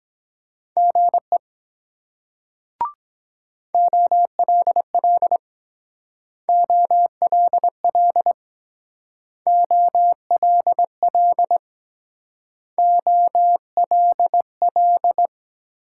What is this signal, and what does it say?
Morse code, heard as a keyed tone, about 21 wpm, 700 Hz. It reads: GE OLL OLL OLL OLL
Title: Speed-Racing Multi-letter Phonemes - 17wpm